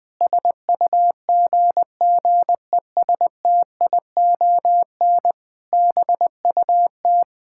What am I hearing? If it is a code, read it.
Morse code, heard as a keyed tone, about 20 wpm, 700 Hz. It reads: SUGGESTION BUT
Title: Sentences from Top 1600 Words 20wpm